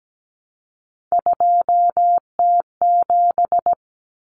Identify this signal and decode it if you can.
Morse code, heard as a keyed tone, about 17 wpm, 690 Hz. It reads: 2T7
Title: Common Cut Numbers - 3-Digit 17wpm